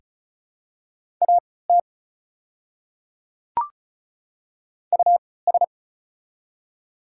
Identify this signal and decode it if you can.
Morse code, heard as a keyed tone, about 34 wpm, spaced as for 20 wpm, 700 Hz. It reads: AT US
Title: 3x Character Spacing - Top 300 Words - 35wpm